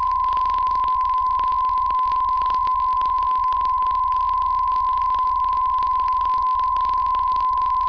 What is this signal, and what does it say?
PSK63, about 62.5 baud.
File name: UVB33-2016-06-11-1500utc-12025khz-byW2IBC